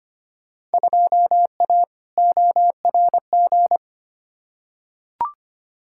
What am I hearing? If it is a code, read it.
Morse code, heard as a keyed tone, about 25 wpm, 700 Hz. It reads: 2A ORG
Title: Field Day Exchanges Encore - 25wpm